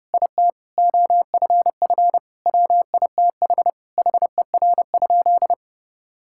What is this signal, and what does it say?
Morse code, heard as a keyed tone, about 32 wpm, 700 Hz. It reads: IT OFF WITH HER?
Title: Sentences from Top 1000 Words - Review Part 6 - 30wpm